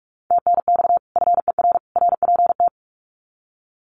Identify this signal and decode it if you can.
Morse code, heard as a keyed tone, about 44 wpm, 700 Hz. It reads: TNX FER RPT
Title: Common QSO Phrases 45wpm